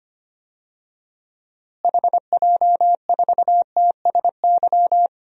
Morse code, heard as a keyed tone, about 25 wpm, 700 Hz. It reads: HJ4TSY
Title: DXCC-100 Call Signs - 6 Characters - Repeated 3x - 25wpm